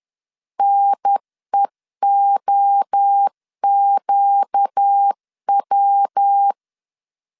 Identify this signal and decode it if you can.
Morse code, heard as a keyed tone, about 10 wpm, 800 Hz. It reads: NEOQW
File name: CW / DET / 1 6 1500C S C 10wpm 4252